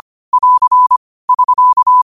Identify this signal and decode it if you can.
Morse code, heard as a keyed tone, about 25 wpm, 1000 Hz. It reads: <AR> <SK>
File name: qotdfiles / 25wpm